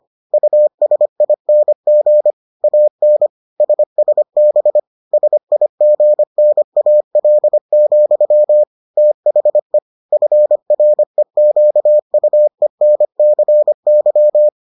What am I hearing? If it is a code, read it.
Morse code, heard as a keyed tone, about 25 wpm, 600 Hz. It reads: USING AN SSB SIGNAL, THE FREQUENCY